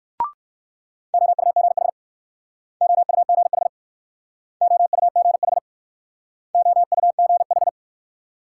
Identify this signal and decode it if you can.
Morse code, heard as a keyed tone, about 59 wpm, 700 Hz. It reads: OUGH OUGH OUGH OUGH